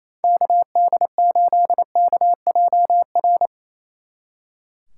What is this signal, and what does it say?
Morse code, heard as a keyed tone, about 33 wpm, 700 Hz. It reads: KD8KJR
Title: Speed Racing - USA 2x3 Call Signs 28wpm